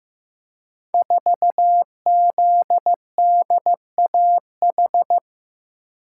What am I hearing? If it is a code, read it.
Morse code, heard as a keyed tone, about 15 wpm, 700 Hz. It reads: 4ZDAH
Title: Code Groups - 5 Letter:Numbers - 15wpm